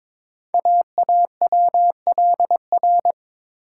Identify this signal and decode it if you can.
Morse code, heard as a keyed tone, about 23 wpm, 700 Hz. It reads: AAWLR
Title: Code Groups - 5 Letters - 22wpm